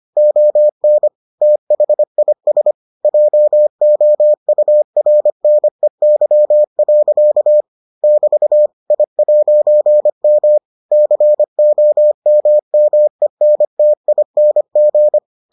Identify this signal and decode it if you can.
Morse code, heard as a keyed tone, about 25 wpm, 600 Hz. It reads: ON THIS JOURNEY. <BT> I'M COMMENTING